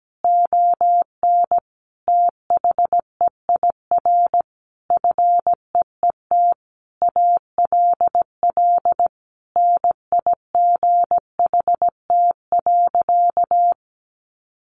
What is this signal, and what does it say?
Morse code, heard as a keyed tone, about 17 wpm, 700 Hz. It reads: ON THEIR FEET ALL NIGHT.
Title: Sentences from Top 1500 Words 17wpm